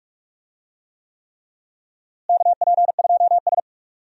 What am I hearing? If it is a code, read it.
Morse code, heard as a keyed tone, about 41 wpm, 700 Hz. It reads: KP2S
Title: DXCC-100 Call Signs - 4 Characters - Repeated 3x - 45wpm